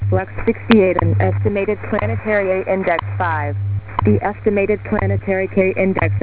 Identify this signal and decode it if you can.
Radio time signal, with speech.
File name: WebSDR records / WWV 5000.0kHz